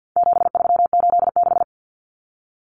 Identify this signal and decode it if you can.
Morse code, heard as a keyed tone, about 50 wpm, 700 Hz. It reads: 7386
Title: Common Cut Numbers - 4-Digit 50wpm